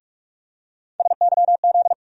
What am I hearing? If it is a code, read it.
Morse code, heard as a keyed tone, about 44 wpm, 700 Hz. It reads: SYZ
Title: USA Call Sign Postfixes - 3 Letters 45wpm